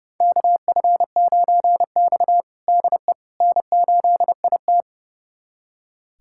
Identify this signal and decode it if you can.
Morse code, heard as a keyed tone, about 30 wpm, 700 Hz. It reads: KF9X DE N8IT
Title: USA Callsign DE Callsign - 4 Characters - 30wpm